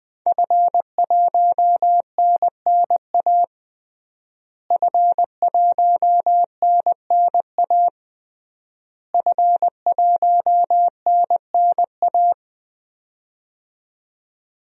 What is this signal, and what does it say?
Morse code, heard as a keyed tone, about 20 wpm, 700 Hz. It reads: F1NNA F1NNA F1NNA
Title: DXCC-100 Call Signs - 5 Characters - Repeated 3x - 20wpm